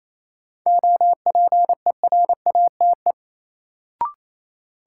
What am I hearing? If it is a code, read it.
Morse code, heard as a keyed tone, about 28 wpm, 700 Hz. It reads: OPERATE
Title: Sentences from Top 1500 Words - New Word Drill - 28wpm